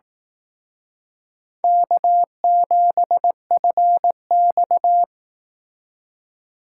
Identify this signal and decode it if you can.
Morse code, heard as a keyed tone, about 19 wpm, 700 Hz. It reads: K7FX
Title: Speed Racing - USA 1x2 Call Signs 15wpm